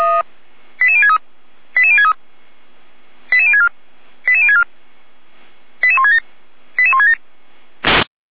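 Five-tone selective call (ZVEI), heard as quick runs of five tones, tones of about 70 ms. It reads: 88063 88063 88064 88064 88167 88167
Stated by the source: ZVEI five-tone selective call received via SDR.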